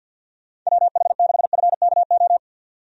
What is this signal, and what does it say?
Morse code, heard as a keyed tone, about 50 wpm, 700 Hz. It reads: WH6FKO
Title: USA Call Signs - 6 Characters - 50wpm